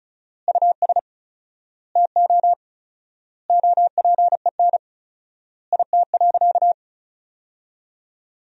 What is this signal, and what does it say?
Morse code, heard as a keyed tone, about 34 wpm, 700 Hz. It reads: US TO OPEN IT.
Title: 4x Word-Spacing - Sentences from Top 300 Words - 35wpm